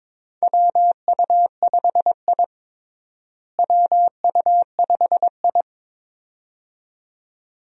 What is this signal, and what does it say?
Morse code, heard as a keyed tone, about 22 wpm, 700 Hz. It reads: WU5I WU5I
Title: USA Call Signs - 4 Characters - Repeated 2x - 22wpm